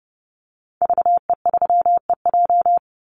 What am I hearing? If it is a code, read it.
Morse code, heard as a keyed tone, about 30 wpm, 700 Hz. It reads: VE3EJ
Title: Call Signs Repeated 2x - 30wpm